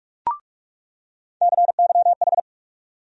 Morse code, heard as a keyed tone, about 44 wpm, 700 Hz. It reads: CYH
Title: USA Call Sign Postfixes - 3 Letters 45wpm